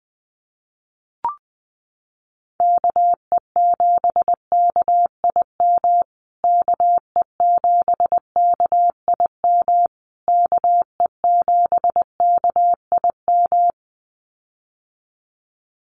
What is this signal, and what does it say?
Morse code, heard as a keyed tone, about 19 wpm, 700 Hz. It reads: KE7KIM KE7KIM KE7KIM
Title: USA Call Signs Repeated 3x 20wpm